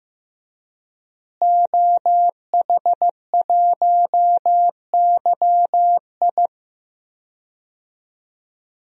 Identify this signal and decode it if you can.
Morse code, heard as a keyed tone, about 15 wpm, 700 Hz. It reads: OH1YI
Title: DXCC-100 Call Signs - 5 Characters - Repeated 2x - 15wpm